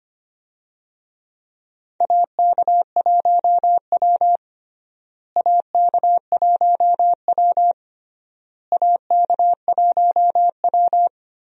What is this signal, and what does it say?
Morse code, heard as a keyed tone, about 25 wpm, 700 Hz. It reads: AK1W AK1W AK1W
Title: USA Call Signs - 4 Characters - Repeated 3x - 25wpm